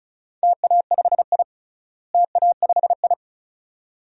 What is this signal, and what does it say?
Morse code, heard as a keyed tone, about 34 wpm, 700 Hz. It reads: TA5I TA5I
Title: DXCC-100 Call Signs - 4 Characters - Repeated 2x - 35wpm